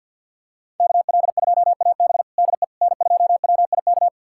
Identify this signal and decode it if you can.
Morse code, heard as a keyed tone, about 52 wpm, 700 Hz. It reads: KL2AB DE N2WIK
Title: USA Callsign DE Callsign - 5 Characters - 50wpm